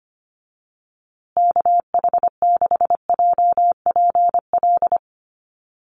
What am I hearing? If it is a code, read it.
Morse code, heard as a keyed tone, about 25 wpm, 700 Hz. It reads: KH6JPL
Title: USA Call Signs Repeated 3x 25wpm